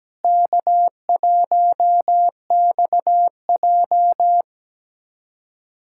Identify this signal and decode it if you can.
Morse code, heard as a keyed tone, about 17 wpm, 700 Hz. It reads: K1XJ
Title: USA Callsign DE Callsign - 4 Characters - 17wpm